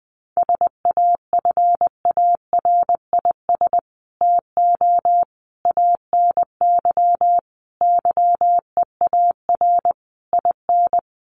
Morse code, heard as a keyed tone, about 19 wpm, 700 Hz. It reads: SAFARIS TO ANY YEAR IN
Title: A Sound Of Thunder 20wpm